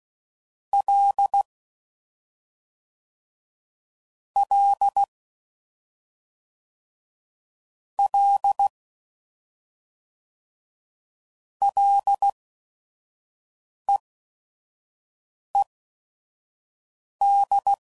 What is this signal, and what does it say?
Morse code, heard as a keyed tone, about 16 wpm, spaced as for 5 wpm, 800 Hz. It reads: L L L LEED
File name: Prefixes / FLs & THs / Single L's